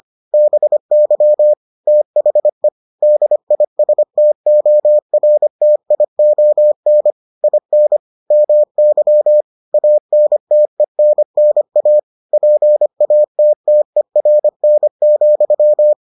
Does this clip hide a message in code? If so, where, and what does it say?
Yes: Morse code from 0.3 to 15.9 s, heard as a keyed tone, about 25 wpm, 600 Hz. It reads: BY THE DISTORTION IN MY ANTENNA PATTERN,